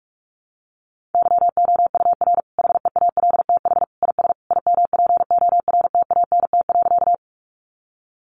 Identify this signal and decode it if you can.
Morse code, heard as a keyed tone, about 44 wpm, 700 Hz. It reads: YOUR HEALTH IS IMPORTANT.
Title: Sentences from Top 600 Words 45wpm